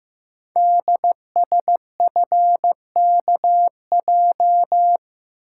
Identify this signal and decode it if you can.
Morse code, heard as a keyed tone, about 15 wpm, 700 Hz. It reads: DSFKJ
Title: Code Groups - 5 Letters - 15wpm